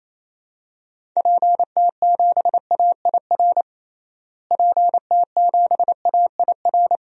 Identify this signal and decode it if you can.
Morse code, heard as a keyed tone, about 28 wpm, 700 Hz. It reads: PT7AIR PT7AIR
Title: DXCC-100 Call Signs - 6 Characters - Repeated 2x - 28wpm